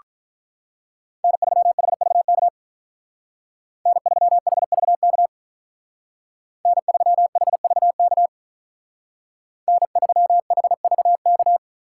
Morse code, heard as a keyed tone, about 44 wpm, 700 Hz. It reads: N3HVK N3HVK N3HVK N3HVK
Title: Speed Racing - USA Call Signs 35wpm